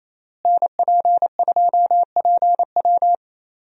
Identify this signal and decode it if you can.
Morse code, heard as a keyed tone, about 28 wpm, 700 Hz. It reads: NP2PW
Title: USA Call Signs - 5 Characters - Repeated 3x - 28wpm